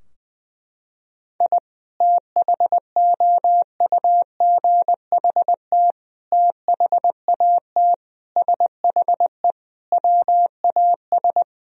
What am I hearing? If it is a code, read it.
Morse code, heard as a keyed tone, about 21 wpm, 700 Hz. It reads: I THOUGHT THAT SHE WAS
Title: Sentences from Top 1000 Words - Review Part 3 - 20wpm